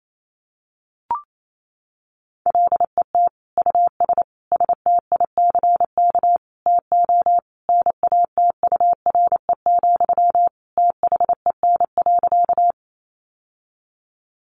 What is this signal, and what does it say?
Morse code, heard as a keyed tone, about 28 wpm, 700 Hz. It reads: LET US STICK TO NATURE, THEN.
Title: Sentences from Top 1200 Words 28wpm